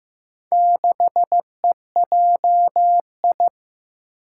Morse code, heard as a keyed tone, about 15 wpm, 700 Hz. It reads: 6EJI
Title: Four Challenging Letter-Number Combinations 15wpm